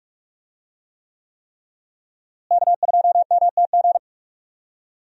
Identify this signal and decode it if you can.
Morse code, heard as a keyed tone, about 45 wpm, 700 Hz. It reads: K2MTG
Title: USA Call Signs - 5 Characters - Repeated 3x - 45wpm